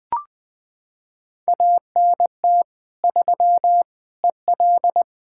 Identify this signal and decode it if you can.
Morse code, heard as a keyed tone, about 20 wpm, 700 Hz. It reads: ANT 3 EL
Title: Practice QSO - Exchange 2 - Info - 20wpm